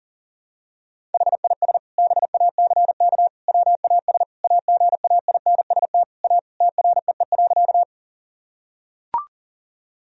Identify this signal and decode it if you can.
Morse code, heard as a keyed tone, about 40 wpm, 700 Hz. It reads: HIS BACK WAS AGAINST A TREE.
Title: Sentences from Top 1000 Words - Review Part 1 - 40wpm